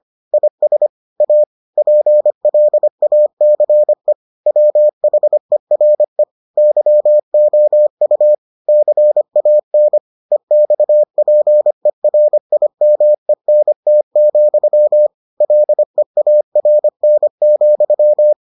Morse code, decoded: IS A PLACE WHERE YOU CAN EXPERIMENT, LEARN,

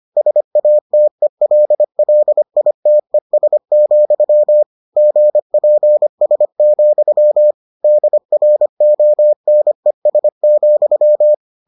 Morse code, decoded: SATELLITES, GPS, DRONES,

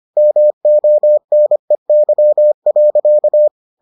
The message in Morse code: MONEY.

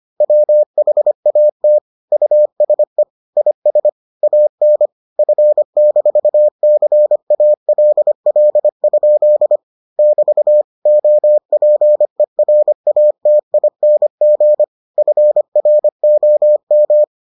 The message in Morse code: WHAT USE IS AN F-CALL? <BT> OPERATING FROM

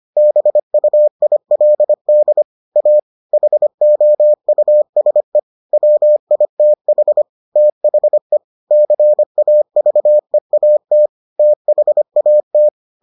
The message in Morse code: BUILD A HOUSE WITH THE CAVEAT THAT

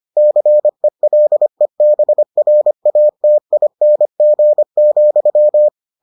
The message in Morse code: CELEBRATING,

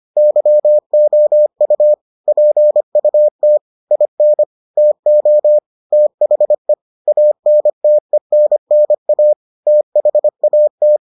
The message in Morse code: YOU PUT IN TO THE ANTENNA THAT

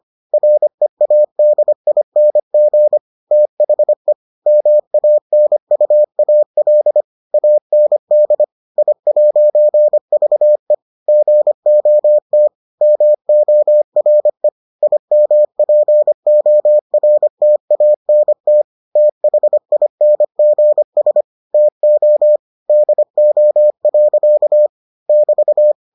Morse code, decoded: READING THE MANUAL AND I'VE GOT MORE IMPORTANT THINGS TO DO. <BT>

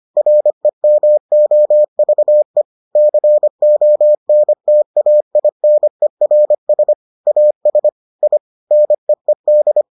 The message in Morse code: REMOVE CONTAINERS AS I NEED

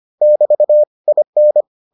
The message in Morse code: <BT> IN

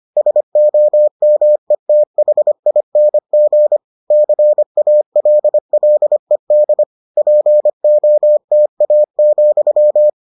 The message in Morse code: SOMETHING CALLED POTA,

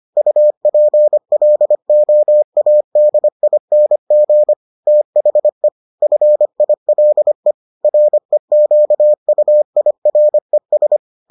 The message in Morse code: UPLOADING THE FILE REQUIRES

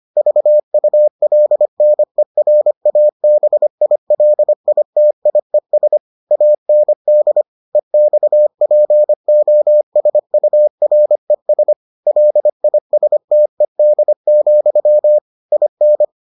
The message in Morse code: VULNERABILITIES AND EXPOSURES LISTED, IN

